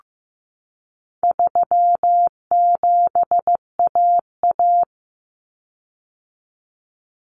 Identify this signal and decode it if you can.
Morse code, heard as a keyed tone, about 15 wpm, 700 Hz. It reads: 37AA
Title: Common Cut Numbers - 4-Digit 15wpm